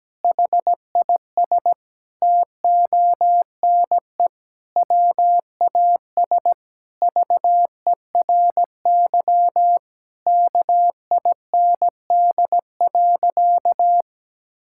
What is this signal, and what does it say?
Morse code, heard as a keyed tone, about 17 wpm, 700 Hz. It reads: HIS TONE WAS VERY KIND.
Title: Sentences from Top 1600 Words 17wpm